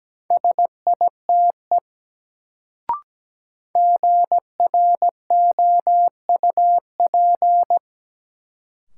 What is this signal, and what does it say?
Morse code, heard as a keyed tone, about 17 wpm, 700 Hz. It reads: SITE GROUP
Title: Sentences from Top 1000 Words - Review Part 1 - New Word Drill - 17wpm